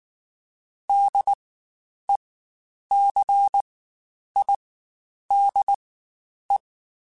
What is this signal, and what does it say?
Morse code, heard as a keyed tone, about 19 wpm, spaced as for 9 wpm, 800 Hz. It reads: DECIDE
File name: Spaced / Chuckles Spaced-mp3s / HalfFull sp @18